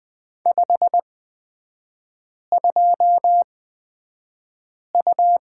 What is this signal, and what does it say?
Morse code, heard as a keyed tone, about 20 wpm, spaced as for 5 wpm, 700 Hz. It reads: 52U
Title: 224 Lesson 36e - BT - Three Characters 20wpm